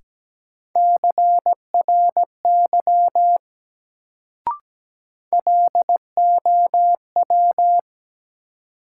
Morse code, heard as a keyed tone, about 17 wpm, 700 Hz. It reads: CRY LOW
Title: Sentences from Top 500 Words - New Word Drill - 17wpm